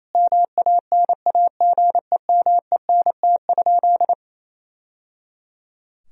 Morse code, heard as a keyed tone, about 28 wpm, 700 Hz. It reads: MANAGEMENT?
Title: Sentences from Top 1000 Words - Review Part 4 - 28wpm